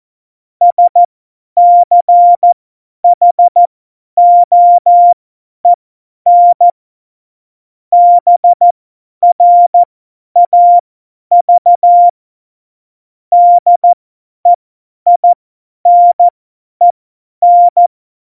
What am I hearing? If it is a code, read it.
Morse code, heard as a keyed tone, about 14 wpm, spaced as for 10 wpm, 700 Hz. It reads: SCHOEN BRAV DEINEN
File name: Morsekurs / Witze / Witz 20. Februar 2018, 50 F70